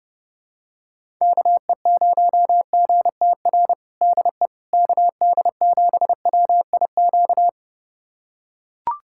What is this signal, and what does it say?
Morse code, heard as a keyed tone, about 29 wpm, 700 Hz. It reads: KE0GTR DE KD7WIQ
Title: USA Callsign DE Callsign - 6 Characters - 30wpm